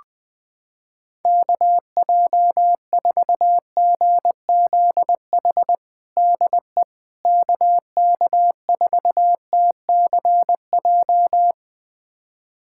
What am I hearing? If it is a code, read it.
Morse code, heard as a keyed tone, about 20 wpm, 700 Hz. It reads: KJ4GZH DE KK4TCJ
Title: USA Callsign DE Callsign - 6 Characters - 20wpm